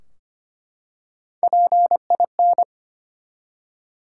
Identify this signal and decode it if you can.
Morse code, heard as a keyed tone, about 25 wpm, 700 Hz. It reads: PIN
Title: Three Letter Words - 25wpm